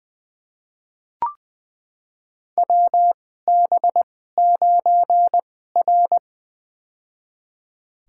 Morse code, heard as a keyed tone, about 21 wpm, spaced as for 15 wpm, 700 Hz. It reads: WB9R
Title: USA Call Signs - 4 Characters - 2x Character Spacing - 20wpm